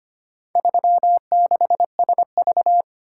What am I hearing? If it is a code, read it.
Morse code, heard as a keyed tone, about 25 wpm, 690 Hz. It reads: 36SV
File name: practice / 25wpm / Four Letter-Number Combinations - 25wpm